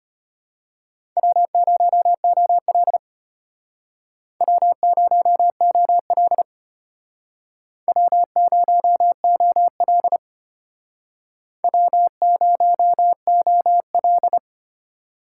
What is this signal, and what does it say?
Morse code, heard as a keyed tone, about 33 wpm, 700 Hz. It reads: W0OL W0OL W0OL W0OL